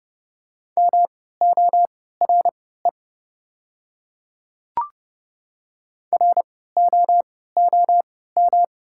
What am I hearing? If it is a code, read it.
Morse code, heard as a keyed tone, about 32 wpm, spaced as for 17 wpm, 690 Hz. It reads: MORE ROOM
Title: 3x Character Spacing - Top 300 Words - 30wpm